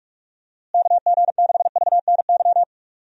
Morse code, heard as a keyed tone, about 46 wpm, 700 Hz. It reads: KG6VNY
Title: USA Call Signs - 6 Characters - Repeated 3x - 45wpm